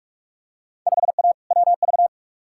Morse code, heard as a keyed tone, about 44 wpm, 700 Hz. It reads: 5A WV
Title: Field Day Exchanges - 45wpm